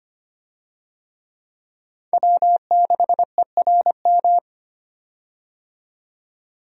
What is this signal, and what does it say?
Morse code, heard as a keyed tone, about 25 wpm, 700 Hz. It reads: W6ERM
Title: USA Call Signs - 5 Characters - 25wpm